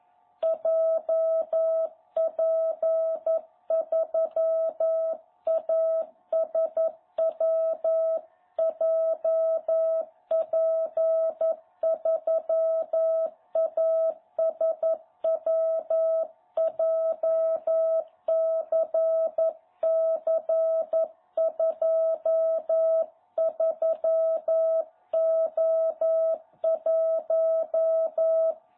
Morse code, heard as a keyed tone, about 11 wpm, 660 Hz. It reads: JP3ASW JP3ASW JCC23O1
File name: jh8cbh / 2021cw